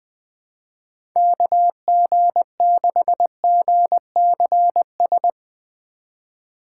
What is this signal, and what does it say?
Morse code, heard as a keyed tone, about 20 wpm, 700 Hz. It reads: KG6GCS
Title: USA Call Signs - 6 Characters - 4x Character Spacing - 20wpm